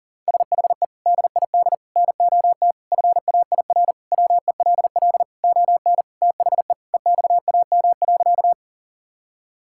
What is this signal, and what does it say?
Morse code, heard as a keyed tone, about 40 wpm, 700 Hz. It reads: SHE DID NOT FAIR WELL ON THE EXAM.
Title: Sentences from Top 1000 Words - Review Part 5 - 40wpm